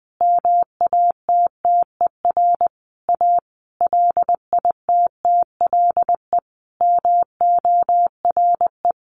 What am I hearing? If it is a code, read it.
Morse code, heard as a keyed tone, about 19 wpm, 700 Hz. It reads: MATTER A LITTLE MORE
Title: Sentences from Top 1400 Words 20wpm